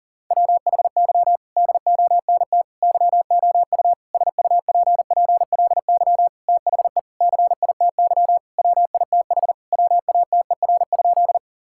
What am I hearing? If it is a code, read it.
Morse code, heard as a keyed tone, about 40 wpm, 700 Hz. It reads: WHY DONT YOU SUPPLY THE CITY WITH WATER?
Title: Sentences from Top 1000 Words - Review Part 5 - 40wpm